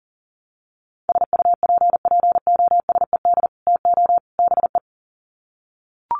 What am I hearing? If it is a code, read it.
Morse code, heard as a keyed tone, about 40 wpm, 700 Hz. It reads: SUPPOSED TO BE
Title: Sets of 3 Words 40wpm